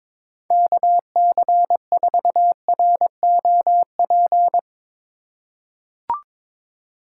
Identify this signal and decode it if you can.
Morse code, heard as a keyed tone, about 22 wpm, 700 Hz. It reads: KC4ROP
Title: USA Call Signs - 6 Characters - 4x Character Spacing - 22wpm